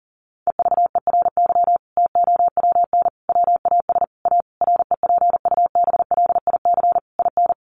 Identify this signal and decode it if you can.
Morse code, heard as a keyed tone, about 40 wpm, 700 Hz. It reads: EVERY TOWN WAS A REPUBLIC IN
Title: Sentences from Top 1500 Words 40wpm